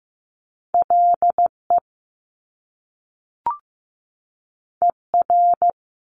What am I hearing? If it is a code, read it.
Morse code, heard as a keyed tone, about 15 wpm, 700 Hz. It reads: LE ER
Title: Top Two Letter Combinations in Words 15wpm